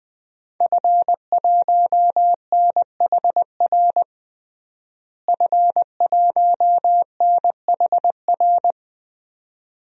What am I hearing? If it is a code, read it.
Morse code, heard as a keyed tone, about 20 wpm, 700 Hz. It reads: F1NHR F1NHR
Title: DXCC-100 Call Signs - 5 Characters - Repeated 2x - 20wpm